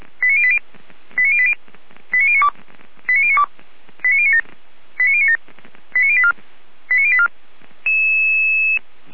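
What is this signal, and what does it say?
Five-tone selective call (ZVEI), heard as quick runs of five tones, tones of about 70 ms. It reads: 89080 89080 89092 89092 89097 89097 89084 89084
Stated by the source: ZVEI five-tone call received by SDR, from feuerwehr-orlamuende.org fire brigade audio archive.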